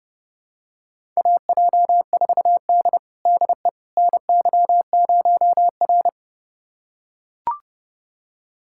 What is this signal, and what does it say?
Morse code, heard as a keyed tone, about 30 wpm, 700 Hz. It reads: AJ4D DE NY0R
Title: USA Callsign DE Callsign - 4 Characters - 30wpm